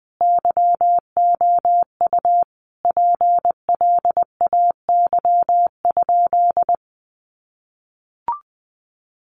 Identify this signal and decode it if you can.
Morse code, heard as a keyed tone, about 20 wpm, 700 Hz. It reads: YOU PLAY?
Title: Sentences from Top 1400 Words 20wpm